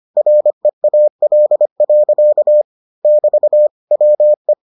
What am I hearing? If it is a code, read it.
Morse code, heard as a keyed tone, about 25 wpm, 600 Hz. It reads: REAL. <BT> WE